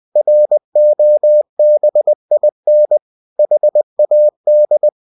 Morse code, heard as a keyed tone, about 20 wpm, 600 Hz. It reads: ROBIN HAD
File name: cw-books / winnie-the-pooh / winnie-20wpm-page-0103